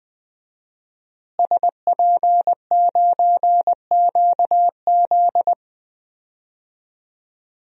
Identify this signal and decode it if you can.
Morse code, heard as a keyed tone, about 20 wpm, 700 Hz. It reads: SP9QZ
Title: DXCC-100 Call Signs - 5 Characters - Repeated 3x - 20wpm